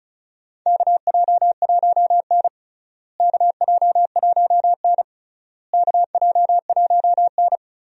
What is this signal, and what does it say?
Morse code, heard as a keyed tone, about 34 wpm, 700 Hz. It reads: KJ1N KJ1N KJ1N
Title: USA Call Signs - 4 Characters - Repeated 3x - 35wpm